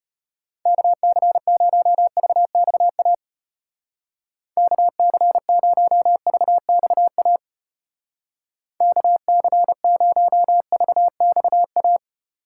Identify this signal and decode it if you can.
Morse code, heard as a keyed tone, about 31 wpm, 700 Hz. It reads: KC0VXA KC0VXA KC0VXA
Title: Speed Racing - USA 2x3 Call Signs 25wpm